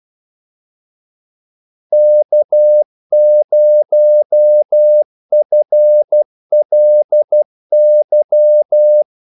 Morse code, heard as a keyed tone, about 12 wpm, 600 Hz. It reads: K0FLY
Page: Kaldesignaler 12 wpm